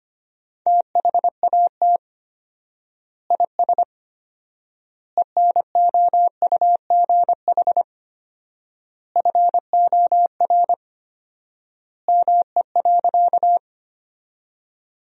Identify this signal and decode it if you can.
Morse code, heard as a keyed tone, about 25 wpm, 700 Hz. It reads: THAT IS ENOUGH FOR ME.
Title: 4x Word-Spacing - Sentences from Top 400 Words - 25wpm